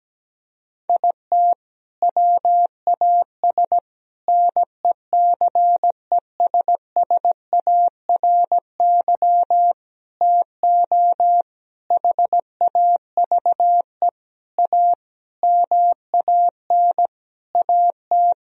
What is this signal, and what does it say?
Morse code, heard as a keyed tone, about 17 wpm, 700 Hz. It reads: IT WAS NECESSARY TO HAVE A MAN AT
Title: Sentences from Top 1000 Words - Review Part 1 - 17wpm